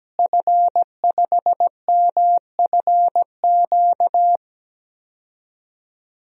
Morse code, heard as a keyed tone, about 17 wpm, 700 Hz. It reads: F5MFQ
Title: DXCC-100 Call Signs - 5 Characters - Repeated 2x - 17wpm